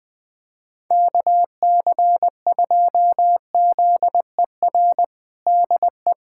Morse code, heard as a keyed tone, about 20 wpm, 700 Hz. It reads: KC2ZER DE
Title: USA Callsign DE Callsign 20wpm